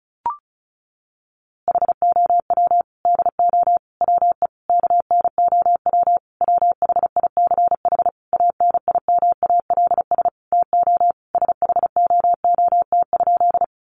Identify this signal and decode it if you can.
Morse code, heard as a keyed tone, about 34 wpm, 700 Hz. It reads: HOW DO WE KNOW WHICH ANIMALS TO SHOOT?
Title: A Sound Of Thunder 35wpm